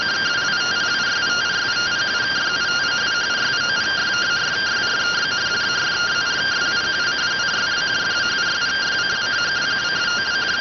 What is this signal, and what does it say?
Two-tone teleprinter signal, about 170 Hz shift.